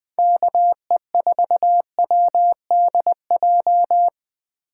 Morse code, heard as a keyed tone, about 20 wpm, 700 Hz. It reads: KE4WDJ
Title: Speed Racing - USA Call Signs 17wpm